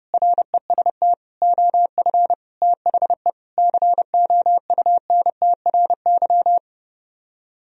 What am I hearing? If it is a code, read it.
Morse code, heard as a keyed tone, about 31 wpm, 700 Hz. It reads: REST OF THE COUNTRY